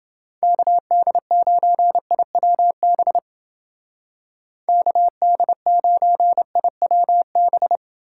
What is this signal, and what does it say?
Morse code, heard as a keyed tone, about 29 wpm, 700 Hz. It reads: KD9IWB KD9IWB